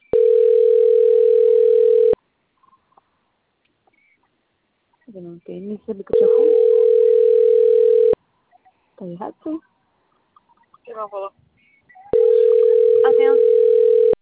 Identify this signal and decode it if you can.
Ringback tone (North America), 440 + 480 Hz, with speech.
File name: out- - - - - .76557